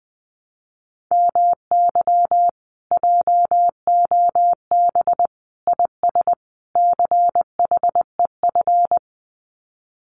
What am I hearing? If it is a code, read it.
Morse code, heard as a keyed tone, about 20 wpm, 700 Hz. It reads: MY JOB IS CHEF
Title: Common QSO Phrases Encore 20wpm